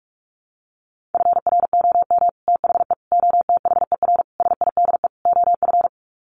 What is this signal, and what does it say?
Morse code, heard as a keyed tone, about 44 wpm, 700 Hz. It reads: FROM THE OTHER SIDE OF
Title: Sets of 5 Words Encore 45wpm